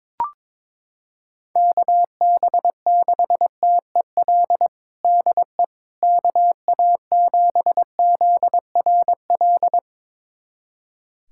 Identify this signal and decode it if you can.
Morse code, heard as a keyed tone, about 22 wpm, 700 Hz. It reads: KB6TEL DE KA7ZRL
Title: USA Callsign DE Callsign 22wpm